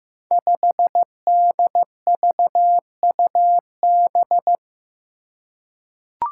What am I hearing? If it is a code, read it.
Morse code, heard as a keyed tone, about 15 wpm, 700 Hz. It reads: 5DVUB
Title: Five Challenging Letter-Number Combinations 15wpm